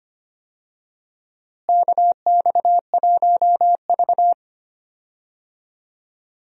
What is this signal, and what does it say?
Morse code, heard as a keyed tone, about 25 wpm, 700 Hz. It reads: KX1V
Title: USA Call Signs - 4 Characters - Repeated 3x - 25wpm